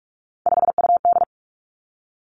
Morse code, heard as a keyed tone, about 46 wpm, 700 Hz. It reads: 5UD
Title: Three Challenging Letter-Number Combinations 45wpm